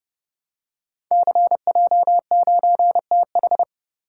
Morse code, heard as a keyed tone, about 30 wpm, 700 Hz. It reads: CJ9TH
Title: Code Groups - 5 Letter:Numbers - 30wpm